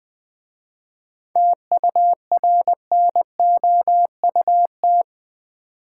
Morse code, heard as a keyed tone, about 20 wpm, 700 Hz. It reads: TURNOUT
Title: 7 Letter Words - First 3 Letters Given - 20wpm